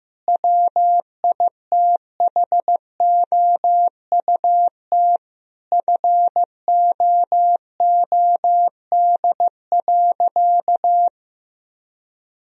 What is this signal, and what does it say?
Morse code, heard as a keyed tone, about 15 wpm, 700 Hz. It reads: WITHOUT FOOD.
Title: Sentences from Top 1000 Words - Review Part 3 - 15wpm